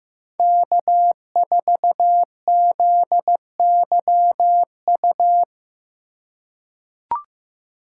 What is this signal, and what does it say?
Morse code, heard as a keyed tone, about 16 wpm, 700 Hz. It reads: K4ZYU
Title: USA Call Signs - 4x Character Spacing - 15wpm